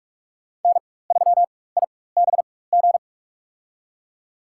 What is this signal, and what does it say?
Morse code, heard as a keyed tone, about 47 wpm, spaced as for 21 wpm, 700 Hz. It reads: N3IBG
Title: USA Call Signs - 5 Characters - 4x Character Spacing - 45wpm